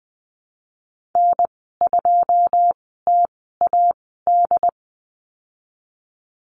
Morse code, heard as a keyed tone, about 21 wpm, spaced as for 15 wpm, 700 Hz. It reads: N2TAD
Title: 122 Lesson 20g - Number 3 - Call Signs 20wpm